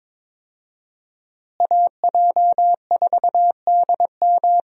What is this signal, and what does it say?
Morse code, heard as a keyed tone, about 21 wpm, 700 Hz. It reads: AJ4DM
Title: USA Call Signs - 5 Characters - Repeated 2x - 22wpm